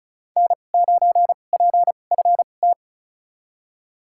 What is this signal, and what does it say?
Morse code, heard as a keyed tone, about 34 wpm, spaced as for 25 wpm, 700 Hz. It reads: N9PFT
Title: USA Call Signs - 5 Characters - 2x Character Spacing - 35wpm